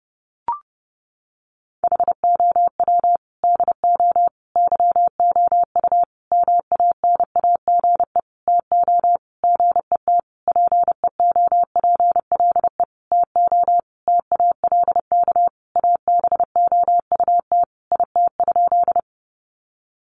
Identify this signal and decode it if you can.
Morse code, heard as a keyed tone, about 28 wpm, 700 Hz. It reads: HOW DO YOU MANAGE TO GET PEOPLE TO TALK ABOUT IT?
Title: Sentences from Top 1400 Words 30wpm